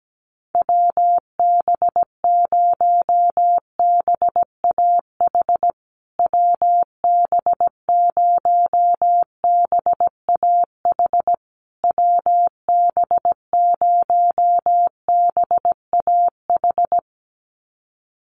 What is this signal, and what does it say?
Morse code, heard as a keyed tone, about 17 wpm, 700 Hz. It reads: WB0BAH WB0BAH WB0BAH
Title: USA Call Signs Repeated 3x 17wpm